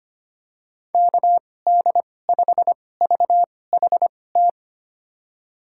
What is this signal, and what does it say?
Morse code, heard as a keyed tone, about 25 wpm, spaced as for 18 wpm, 700 Hz. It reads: KD5VHT
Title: USA Call Signs - 6 Characters - 2x Character Spacing - 25wpm